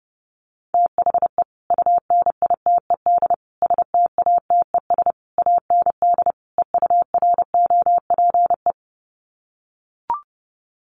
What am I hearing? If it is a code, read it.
Morse code, heard as a keyed tone, about 30 wpm, 700 Hz. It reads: THE UNITED STATES AND EUROPE
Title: Sets of 5 Words Encore 30wpm